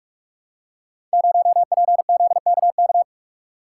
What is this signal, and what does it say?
Morse code, heard as a keyed tone, about 44 wpm, 700 Hz. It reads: 0PZKK
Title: Code Groups - 5 Letter:Numbers - 45wpm